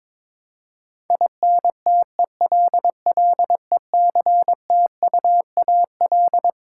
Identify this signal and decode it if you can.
Morse code, heard as a keyed tone, about 21 wpm, 700 Hz. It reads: INTELLECTUAL
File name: Twelve Letter Words - 22wpm